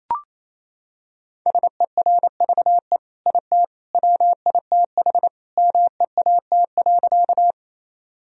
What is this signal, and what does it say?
Morse code, heard as a keyed tone, about 27 wpm, 700 Hz. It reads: SERVE IT WITH MEAT.
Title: Sentences from Top 1000 Words - Review Part 3 - 28wpm